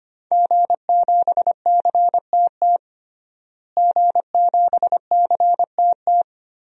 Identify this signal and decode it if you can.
Morse code, heard as a keyed tone, about 25 wpm, 700 Hz. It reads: G7CTT G7CTT
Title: DXCC-100 Call Signs - 5 Characters - Repeated 3x - 25wpm